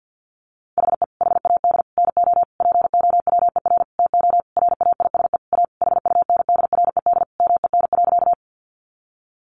Morse code, heard as a keyed tone, about 50 wpm, 700 Hz. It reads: HE HAD NO POWER TO RAISE A HUNDRED MEN.
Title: Sentences from Top 1400 Words 50wpm